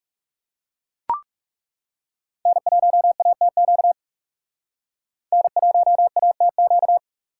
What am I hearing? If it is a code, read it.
Morse code, heard as a keyed tone, about 38 wpm, 700 Hz. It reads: N1ATQ N1ATQ